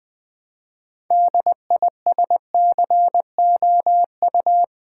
Morse code, heard as a keyed tone, about 20 wpm, 700 Hz. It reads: DISCOU
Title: Build A Word - 3 Letters At A Time - 9-Letter Words - 20wpm